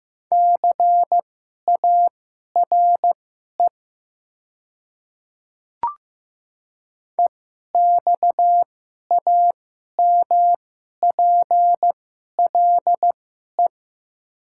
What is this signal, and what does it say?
Morse code, heard as a keyed tone, about 15 wpm, spaced as for 11 wpm, 700 Hz. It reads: CARE EXAMPLE
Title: 2x Character Spacing - Top 300 Words - 15wpm